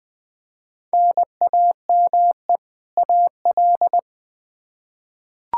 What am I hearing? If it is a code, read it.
Morse code, heard as a keyed tone, about 21 wpm, 700 Hz. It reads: NAME AL
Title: K1USN SST QSO Practice 20wpm